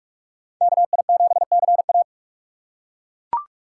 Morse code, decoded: KI7CA

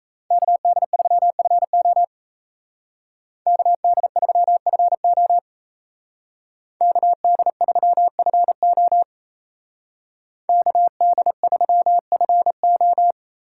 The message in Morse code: KD3FO KD3FO KD3FO KD3FO

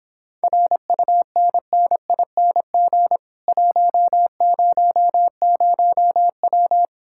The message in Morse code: RUNNING 100W

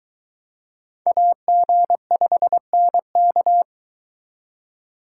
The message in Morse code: AG5NK

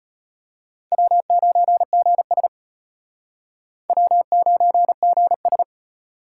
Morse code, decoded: W9GS W9GS

